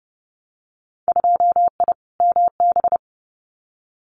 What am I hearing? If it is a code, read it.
Morse code, heard as a keyed tone, about 30 wpm, 700 Hz. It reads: 2I MB
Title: Winter Field Day Exchanges Encore 30wpm